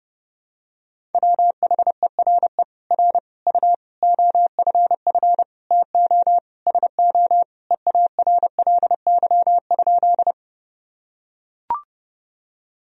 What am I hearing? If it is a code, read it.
Morse code, heard as a keyed tone, about 30 wpm, 700 Hz. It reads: WHERE R U OFF TO SO EARLY?
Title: Sentences from Top 400 Words and Abbreviations 30wpm